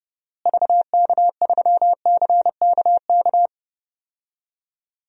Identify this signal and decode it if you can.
Morse code, heard as a keyed tone, about 30 wpm, 700 Hz. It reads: VK3CKK
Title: DXCC-100 Call Signs - 6 Characters - Repeated 3x - 30wpm